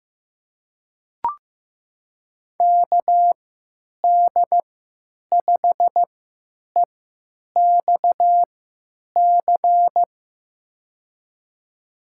Morse code, heard as a keyed tone, about 15 wpm, spaced as for 9 wpm, 700 Hz. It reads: KD5EXC
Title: USA Call Signs - 6 Characters - 3x Character Spacing - 15wpm